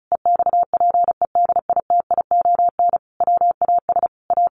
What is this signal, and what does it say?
Morse code, heard as a keyed tone, about 34 wpm, 700 Hz. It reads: EXPEDITION WAS A
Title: A Sound Of Thunder 35wpm